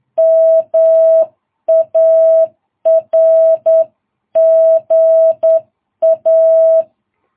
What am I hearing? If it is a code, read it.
Morse code, heard as a keyed tone, about 10 wpm, 660 Hz. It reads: MARGA
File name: jh8cbh / 2022cw